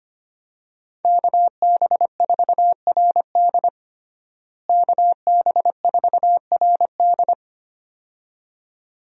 Morse code, heard as a keyed tone, about 25 wpm, 700 Hz. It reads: KB4RD KB4RD
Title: USA Call Signs - 5 Characters - Repeated 2x - 25wpm